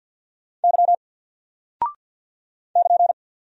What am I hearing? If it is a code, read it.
Morse code, heard as a keyed tone, about 50 wpm, 700 Hz. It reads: Y <KN>
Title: Single Character (Letter:Number:Punctuation:Prosign) 50wpm